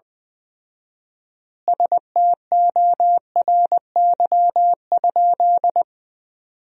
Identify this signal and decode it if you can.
Morse code, heard as a keyed tone, about 20 wpm, 700 Hz. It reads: STORY?
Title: 4x Word-Spacing - Sentences from Top 400 Words - 20wpm